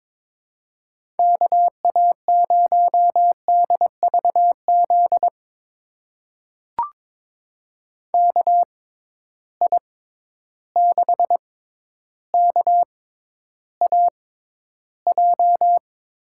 Morse code, heard as a keyed tone, about 22 wpm, spaced as for 11 wpm, 700 Hz. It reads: KA0DVZ KI6KAJ
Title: USA Call Signs - 6 Characters - 6x Character Spacing - 22wpm